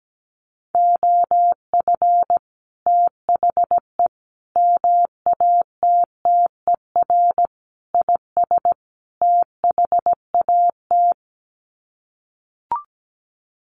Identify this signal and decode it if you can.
Morse code, heard as a keyed tone, about 17 wpm, 700 Hz. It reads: OF THE MATTER IS THAT
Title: Sets of 5 Words Encore 17wpm